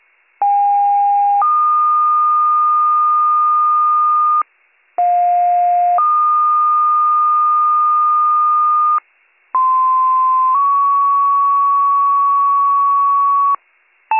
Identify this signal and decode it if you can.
Two-tone page, heard as a one-second tone, then a three-second tone, about 810 then 1220 Hz, about 710 then 1220 Hz, about 1040 then 1120 Hz. It is heard from 0.4 to 13.6 s.